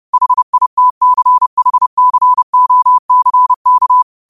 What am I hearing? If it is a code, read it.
Morse code, heard as a keyed tone, about 30 wpm, 1000 Hz. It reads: HITCHCOCK
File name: qotdfiles / 30wpm